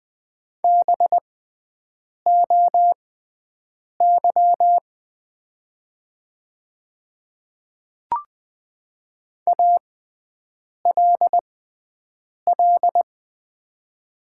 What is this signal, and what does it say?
Morse code, heard as a keyed tone, about 20 wpm, spaced as for 7 wpm, 700 Hz. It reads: BOY ALL
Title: 6x Character Spacing - Top 200 Words - 20wpm